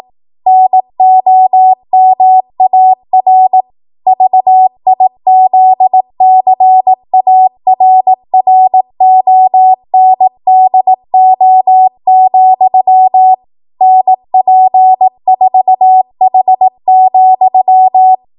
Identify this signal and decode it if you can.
Morse code, heard as a keyed tone, about 18 wpm, 750 Hz. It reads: NOMAR VIZCARRONDO, NP4H,
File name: Archive / 18 WPM / 18WPM